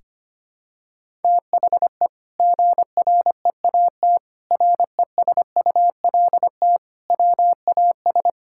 Morse code, heard as a keyed tone, about 25 wpm, 700 Hz. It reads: THE GREAT RESULT WAS
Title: Sentences from Top 1000 Words - Review Part 5 - 25wpm